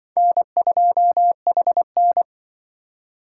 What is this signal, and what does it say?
Morse code, heard as a keyed tone, about 24 wpm, 700 Hz. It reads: N2HN
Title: Speed Racing - USA Call Signs 20wpm